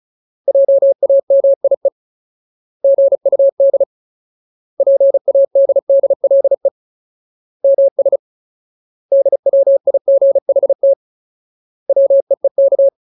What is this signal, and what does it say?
Morse code, heard as a keyed tone, about 34 wpm, 550 Hz. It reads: JAMIE GUD PADDLE MS DWIGHT WEEK
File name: Ultimate Word List / 35WPM Ultimate